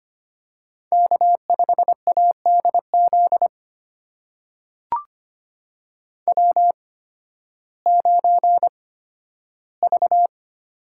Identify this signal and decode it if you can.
Morse code, heard as a keyed tone, about 25 wpm, spaced as for 13 wpm, 700 Hz. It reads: K5ADZ W9V
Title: USA Call Signs - 5 Characters - 8x Character Spacing - 25wpm